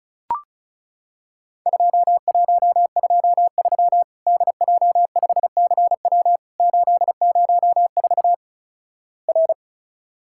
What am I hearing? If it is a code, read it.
Morse code, heard as a keyed tone, about 36 wpm, 700 Hz. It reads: 2123 DJ5CW 804 R
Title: Worked All Europe DX Contest - QTC Exchange - 35wpm